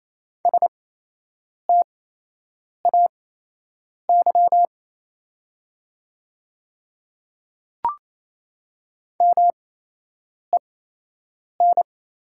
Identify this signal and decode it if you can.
Morse code, heard as a keyed tone, about 28 wpm, spaced as for 8 wpm, 700 Hz. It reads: STAY MEN
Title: 8x Character Spacing - Top 500 Words - 28wpm